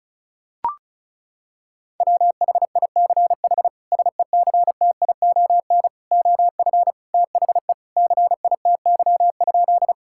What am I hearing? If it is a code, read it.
Morse code, heard as a keyed tone, about 36 wpm, 700 Hz. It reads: WHICH SECTION OF THE CITY?
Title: Sentences from Top 1000 Words - Review Part 5 - 35wpm